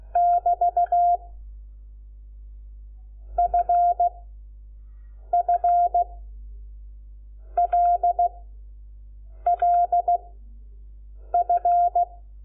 Morse code, heard as a keyed tone, about 16 wpm, spaced as for 6 wpm, 700 Hz. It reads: <BT> FFLLF
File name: Lecciones / CLASE Nº5